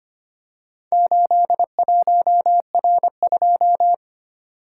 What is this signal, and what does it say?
Morse code, heard as a keyed tone, about 25 wpm, 700 Hz. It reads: 81R2
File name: practice / 25wpm / Four Letter-Number Combinations - 25wpm